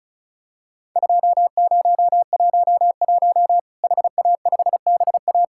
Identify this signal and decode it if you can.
Morse code, heard as a keyed tone, about 36 wpm, 700 Hz. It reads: 2011 HA5BA
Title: Worked All Europe DX Contest - QTC Exchange - 35wpm